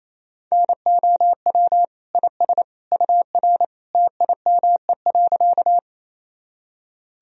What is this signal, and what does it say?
Morse code, heard as a keyed tone, about 28 wpm, 700 Hz. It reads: NOW IS UR TIME.
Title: Sentences from Top 100 Words and Abbreviations 28wpm